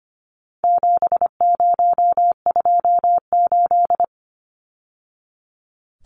Morse code, heard as a keyed tone, about 25 wpm, 700 Hz. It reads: 7028
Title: Numbers - 4-Digit 25wpm